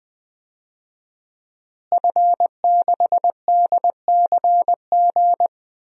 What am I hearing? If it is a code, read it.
Morse code, heard as a keyed tone, about 20 wpm, 700 Hz. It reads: F6DCG
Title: DXCC-100 Call Signs - 5 Characters - Repeated 2x - 20wpm